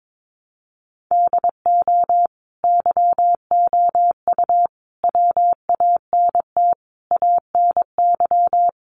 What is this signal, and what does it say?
Morse code, heard as a keyed tone, about 22 wpm, 700 Hz. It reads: DO YOU WANT ANY
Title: Sentences from Top 400 Words 22wpm